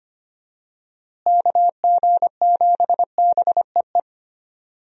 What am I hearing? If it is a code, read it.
Morse code, heard as a keyed tone, about 25 wpm, 700 Hz. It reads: KG7BEE
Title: USA Call Signs - 6 Characters - Repeated 3x - 25wpm